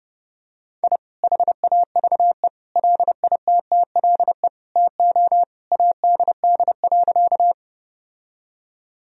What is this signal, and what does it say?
Morse code, heard as a keyed tone, about 30 wpm, 700 Hz. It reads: I HAVE LITTLE TO ADD.
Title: Sentences from Top 1000 Words - Review Part 1 - 30wpm